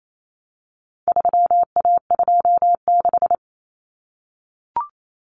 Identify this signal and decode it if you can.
Morse code, heard as a keyed tone, about 28 wpm, 700 Hz. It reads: 3A26
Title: Common Cut Numbers - 4-Digit 28wpm